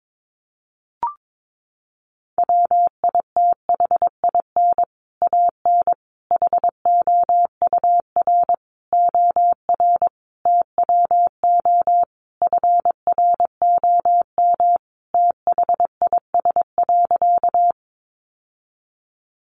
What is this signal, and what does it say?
Morse code, heard as a keyed tone, about 22 wpm, 700 Hz. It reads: WITHIN AN HOUR OR TWO FROM THIS.
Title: Sentences from Top 700 Words 22wpm